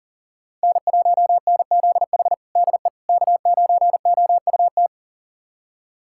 Morse code, heard as a keyed tone, about 37 wpm, 700 Hz. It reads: N1NZH DE K9OUT
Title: USA Callsign DE Callsign - 5 Characters - 40wpm